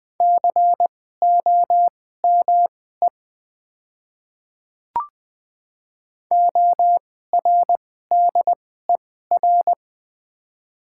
Morse code, heard as a keyed tone, about 20 wpm, spaced as for 14 wpm, 700 Hz. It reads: COME ORDER
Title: 2x Character Spacing - Top 500 Words - 20wpm